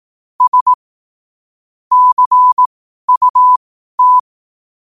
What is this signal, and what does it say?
Morse code, heard as a keyed tone, about 18 wpm, spaced as for 12 wpm, 1000 Hz. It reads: S CUT
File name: qotdfiles / 12wpm